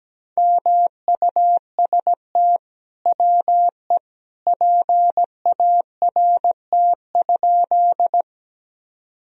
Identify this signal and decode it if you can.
Morse code, heard as a keyed tone, about 17 wpm, 700 Hz. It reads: MUST WE PART?
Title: Sentences from Top 1000 Words - Review Part 1 - 17wpm